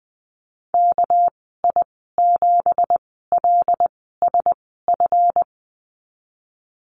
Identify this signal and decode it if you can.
Morse code, heard as a keyed tone, about 19 wpm, spaced as for 14 wpm, 700 Hz. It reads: KI7LSF
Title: 200 Lesson 32g - Letter k - Call Signs 20wpm